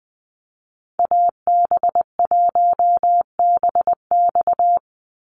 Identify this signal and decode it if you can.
Morse code, heard as a keyed tone, about 20 wpm, 700 Hz. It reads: AB1BX
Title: Call Signs Repeated 3x - 20wpm